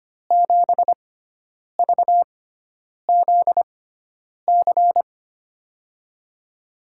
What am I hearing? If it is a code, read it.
Morse code, heard as a keyed tone, about 25 wpm, spaced as for 9 wpm, 700 Hz. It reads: 7VZC
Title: USA Call Signs - 6 Characters - 6x Character Spacing - 25wpm